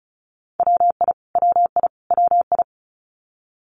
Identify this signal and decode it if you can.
Morse code, heard as a keyed tone, about 34 wpm, 700 Hz. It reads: WI WI WI
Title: US States Repeated 3x 35wpm